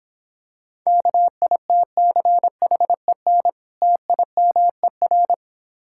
Morse code, heard as a keyed tone, about 26 wpm, 700 Hz. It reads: KITCHEN TIMER